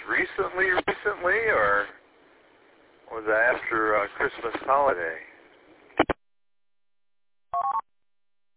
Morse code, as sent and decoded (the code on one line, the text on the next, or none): none